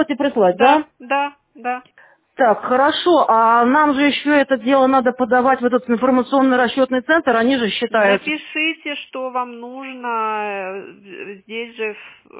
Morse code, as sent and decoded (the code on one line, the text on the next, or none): none